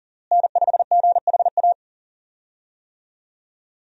Morse code, heard as a keyed tone, about 40 wpm, 700 Hz. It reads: N5GHA
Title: USA Call Signs - 5 Characters - Repeated 2x - 40wpm